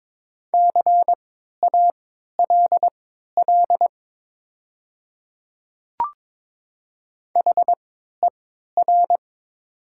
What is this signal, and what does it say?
Morse code, heard as a keyed tone, about 22 wpm, spaced as for 12 wpm, 700 Hz. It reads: CALL HER
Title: 3x Character Spacing - Top 200 Words - 22wpm